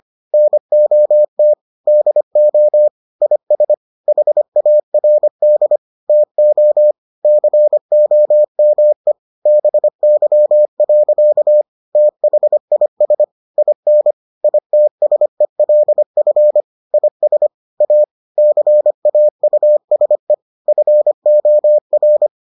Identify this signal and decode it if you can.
Morse code, heard as a keyed tone, about 25 wpm, 600 Hz. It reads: NOT DO IS HARD TO COME BY. THIS IN ITSELF IS A CAUSE FOR